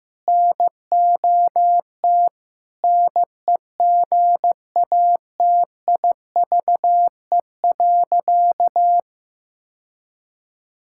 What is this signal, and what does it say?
Morse code, heard as a keyed tone, about 16 wpm, 700 Hz. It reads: NOT NEGATIVE.
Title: Sentences from Top 1000 Words - Review Part 6 - 15wpm